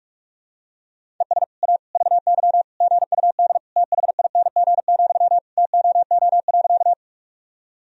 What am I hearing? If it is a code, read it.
Morse code, heard as a keyed tone, about 46 wpm, 700 Hz. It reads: ES A VY GUD THING, TOO.
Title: Sentences from Top 200 Words and Abbreviations 45wpm